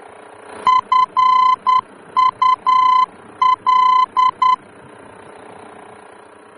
Morse code, heard as a keyed tone, about 10 wpm, 1020 Hz. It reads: FUL
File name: germany / mx / FUL-D-VOR